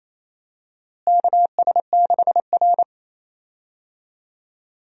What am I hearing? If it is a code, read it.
Morse code, heard as a keyed tone, about 28 wpm, 700 Hz. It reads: KS6R
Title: USA Call Signs - 4 Characters - Repeated 2x - 28wpm